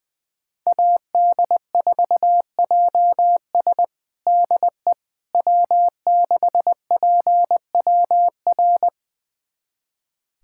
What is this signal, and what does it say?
Morse code, heard as a keyed tone, about 20 wpm, 700 Hz. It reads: AD4JS DE W6PWR
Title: USA Callsign DE Callsign - 5 Characters - 20wpm